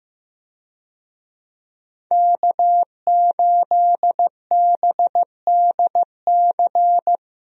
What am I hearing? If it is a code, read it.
Morse code, heard as a keyed tone, about 15 wpm, 700 Hz. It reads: K8BDC
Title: USA Call Signs - 5 Characters - 15wpm